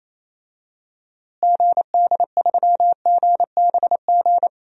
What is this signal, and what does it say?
Morse code, heard as a keyed tone, about 28 wpm, 700 Hz. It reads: GD3GBG
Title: DXCC-100 Call Signs - 6 Characters - Repeated 3x - 28wpm